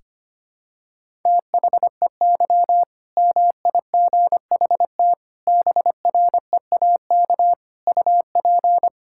Morse code, heard as a keyed tone, about 25 wpm, 700 Hz. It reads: THEY MIGHT BREAK UP
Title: Sentences from Top 1000 Words - Review Part 6 - 25wpm